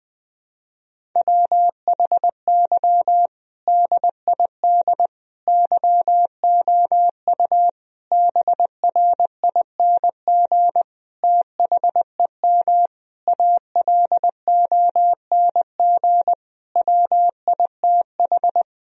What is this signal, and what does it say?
Morse code, heard as a keyed tone, about 20 wpm, 700 Hz. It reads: WHY DID YOU BRING THEM ALONG WITH
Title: Sentences from Top 1000 Words - Review Part 3 - 20wpm